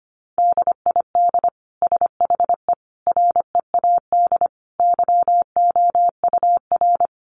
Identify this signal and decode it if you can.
Morse code, heard as a keyed tone, about 25 wpm, 700 Hz. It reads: DID SHE READ YOUR
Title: Sentences from Top 1400 Words 25wpm